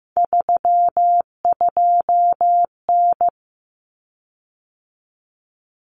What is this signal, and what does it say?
Morse code, heard as a keyed tone, about 15 wpm, 700 Hz. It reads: 32N
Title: Common Cut Numbers - 3-Digit 15wpm